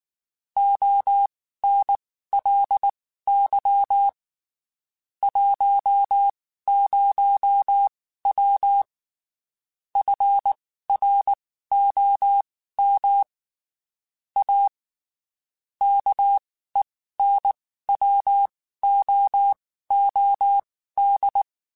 Morse code, heard as a keyed tone, about 19 wpm, spaced as for 14 wpm, 800 Hz. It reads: ONLY 10W FROM A KENWOOD
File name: (spaced) / QSOs Spaced-mp3s / WA7QOX 2 @18 sp